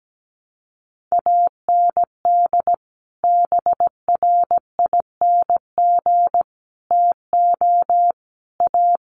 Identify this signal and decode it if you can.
Morse code, heard as a keyed tone, about 17 wpm, 700 Hz. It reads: AND BRING TO A
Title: Sets of 5 Words Encore 17wpm